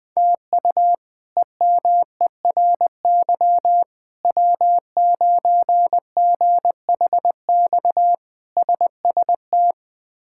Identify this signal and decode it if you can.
Morse code, heard as a keyed tone, about 20 wpm, 700 Hz. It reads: TU EMERY W9GHX SST
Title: K1USN SST QSO Practice - 20wpm